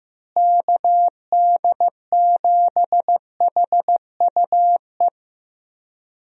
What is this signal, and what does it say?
Morse code, heard as a keyed tone, about 15 wpm, 700 Hz. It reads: KD7HUE
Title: USA Call Signs - 6 Characters - Repeated 3x - 15wpm